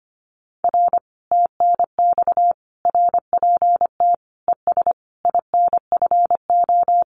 Morse code, heard as a keyed tone, about 25 wpm, 700 Hz. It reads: R TNX RPT ES INFO
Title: Common QSO Phrases Encore 25wpm